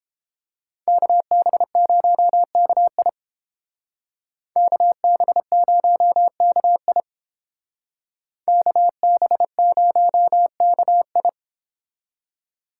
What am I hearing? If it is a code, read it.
Morse code, heard as a keyed tone, about 30 wpm, 700 Hz. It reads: KB0KI KB0KI KB0KI